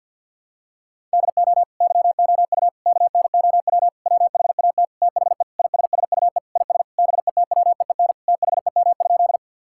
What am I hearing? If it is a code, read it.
Morse code, heard as a keyed tone, about 50 wpm, 700 Hz. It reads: DO YOU KNOW WHAT THE ISSUE IS BETWEEN THEM?
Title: Sentences from Top 1000 Words - Review Part 6 - 50wpm